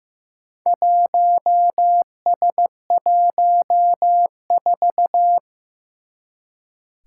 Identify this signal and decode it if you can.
Morse code, heard as a keyed tone, about 15 wpm, 700 Hz. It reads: 1S14
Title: Four Challenging Letter-Number Combinations 15wpm